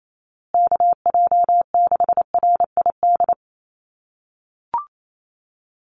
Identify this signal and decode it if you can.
Morse code, heard as a keyed tone, about 28 wpm, 700 Hz. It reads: KJ6RID
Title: USA Call Signs Repeated 3x 28wpm